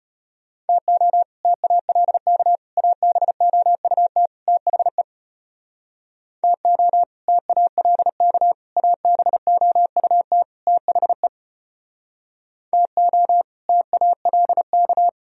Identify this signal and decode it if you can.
Morse code, heard as a keyed tone, about 33 wpm, 700 Hz. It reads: TO TALK ABOUT THE TO TALK ABOUT THE TO TALK